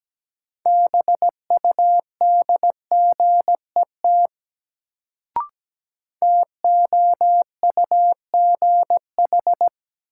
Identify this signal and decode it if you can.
Morse code, heard as a keyed tone, about 17 wpm, 700 Hz. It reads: BUDGET TOUGH
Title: Sentences from Top 1200 Words - New Word Drill - 17wpm